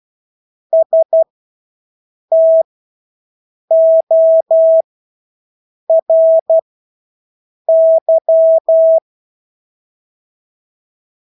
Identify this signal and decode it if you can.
Morse code, heard as a keyed tone, about 12 wpm, spaced as for 6 wpm, 650 Hz. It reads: STORY
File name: Engelske ord 12-6 wpm